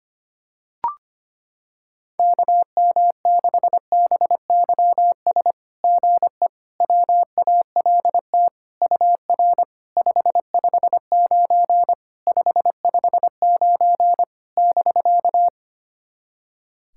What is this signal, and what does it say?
Morse code, heard as a keyed tone, about 25 wpm, 700 Hz. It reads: KM6BYS GE WALT UR 559 559 <BK>
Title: SOTA QSO Practice 25wpm